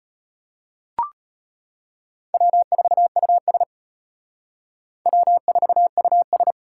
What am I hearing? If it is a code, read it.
Morse code, heard as a keyed tone, about 32 wpm, 700 Hz. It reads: W4US W4US